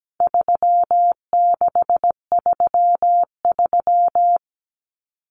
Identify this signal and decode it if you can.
Morse code, heard as a keyed tone, about 17 wpm, 700 Hz. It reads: 3633
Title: Common Cut Numbers - 4-Digit 17wpm